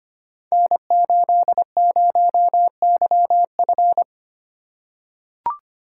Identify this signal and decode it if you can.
Morse code, heard as a keyed tone, about 25 wpm, 700 Hz. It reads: N80YF
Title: Code Groups - 5 Letter:Numbers - 25wpm